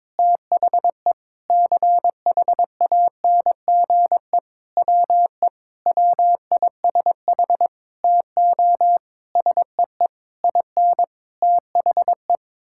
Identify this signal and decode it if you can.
Morse code, heard as a keyed tone, about 22 wpm, 700 Hz. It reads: THE CHANGE WE WISH TO SEE IN THE
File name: Sets of 9 Words - 22wpm